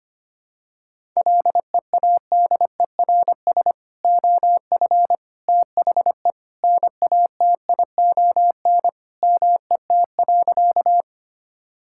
Morse code, heard as a keyed tone, about 25 wpm, 700 Hz. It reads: LEADERS OF THE NATION MET.
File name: practice / 25wpm / Sentences with 3-Word Phrases - 25wpm